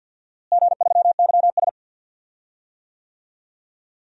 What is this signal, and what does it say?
Morse code, heard as a keyed tone, about 47 wpm, 700 Hz. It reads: G3YS
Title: DXCC-100 Call Signs - 4 Characters - Repeated 2x - 50wpm